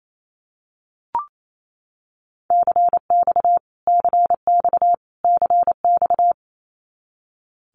Morse code, heard as a keyed tone, about 28 wpm, 700 Hz. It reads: CX CX CX
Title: Abbreviations Repeated 3x - 28wpm